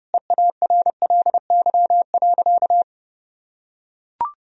Morse code, heard as a keyed tone, about 30 wpm, 700 Hz. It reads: EARLY.
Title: Sentences from Top 1000 Words - Review Part 4 - 30wpm